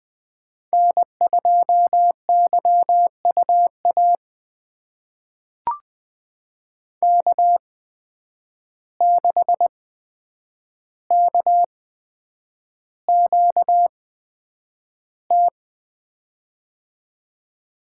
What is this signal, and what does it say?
Morse code, heard as a keyed tone, about 20 wpm, spaced as for 9 wpm, 700 Hz. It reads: N2YUA K6KQT
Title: USA Call Signs - 5 Characters - 8x Character Spacing - 20wpm